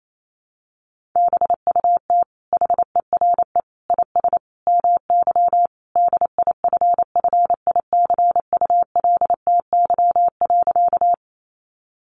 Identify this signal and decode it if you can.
Morse code, heard as a keyed tone, about 28 wpm, 700 Hz. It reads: BUT HERE IS MY DIFFICULTY.
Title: Sentences from Top 1500 Words 28wpm